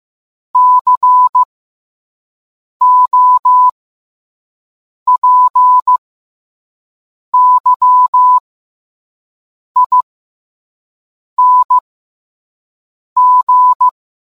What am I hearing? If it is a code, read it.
Morse code, heard as a keyed tone, about 15 wpm, spaced as for 5 wpm, 1000 Hz. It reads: COPYING